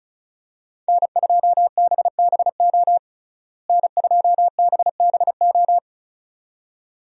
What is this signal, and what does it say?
Morse code, heard as a keyed tone, about 36 wpm, 700 Hz. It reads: N2BBO N2BBO
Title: USA Call Signs - 5 Characters - Repeated 2x - 35wpm